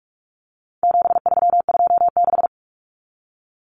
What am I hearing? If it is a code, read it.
Morse code, heard as a keyed tone, about 44 wpm, 700 Hz. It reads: 7326
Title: Common Cut Numbers - 4-Digit 45wpm